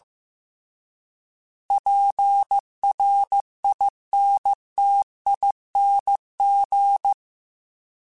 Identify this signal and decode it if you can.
Morse code, heard as a keyed tone, about 15 wpm, 800 Hz. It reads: PRINTING